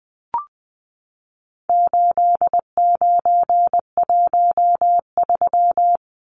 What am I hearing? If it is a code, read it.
Morse code, heard as a keyed tone, about 20 wpm, 700 Hz. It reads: 8913
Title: Numbers - 4-Digit 20wpm